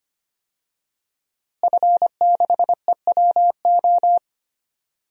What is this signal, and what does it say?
Morse code, heard as a keyed tone, about 25 wpm, 700 Hz. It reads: F6EWO
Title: DXCC-100 Call Signs - 5 Characters - Repeated 3x - 25wpm